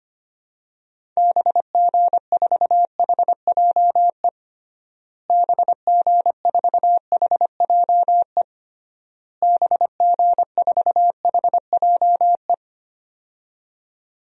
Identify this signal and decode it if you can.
Morse code, heard as a keyed tone, about 25 wpm, 700 Hz. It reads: BG4HJE BG4HJE BG4HJE
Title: DXCC-100 Call Signs - 6 Characters - Repeated 3x - 25wpm